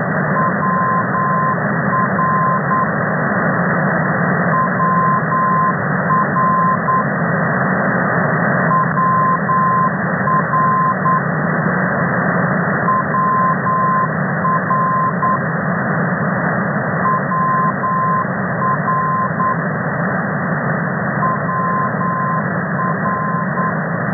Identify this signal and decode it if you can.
Morse code, heard as a keyed tone, about 9 wpm, 1040 Hz, with noise. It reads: WR WR WR WR WR WR
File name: hf ndb